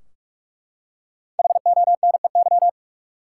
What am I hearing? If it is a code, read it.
Morse code, heard as a keyed tone, about 42 wpm, 700 Hz. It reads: HONEY
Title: Five Letter Words - 45wpm